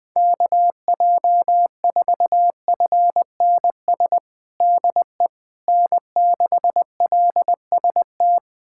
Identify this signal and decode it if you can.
Morse code, heard as a keyed tone, about 21 wpm, 700 Hz. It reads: KJ4FNS DE N6LST
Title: USA Callsign DE Callsign 20wpm